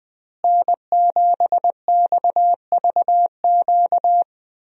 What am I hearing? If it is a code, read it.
Morse code, heard as a keyed tone, about 20 wpm, 700 Hz. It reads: N7XVQ
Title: USA Call Signs - 5 Characters - Repeated 2x - 20wpm